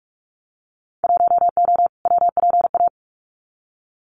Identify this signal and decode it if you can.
Morse code, heard as a keyed tone, about 44 wpm, 700 Hz. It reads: 1O WPA
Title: Winter Field Day Exchanges 45wpm